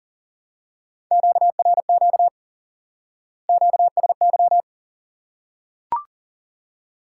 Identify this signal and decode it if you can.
Morse code, heard as a keyed tone, about 40 wpm, 700 Hz. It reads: QRQ QSY
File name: IQE Course 3 - Lesson 89 - YAGI - Two Elements - 40wpm